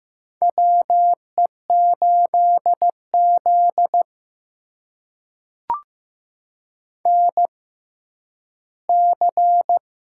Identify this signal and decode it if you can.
Morse code, heard as a keyed tone, about 15 wpm, spaced as for 10 wpm, 700 Hz. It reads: WE8Z NC
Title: USA Call Signs - 6x Character Spacing - 15wpm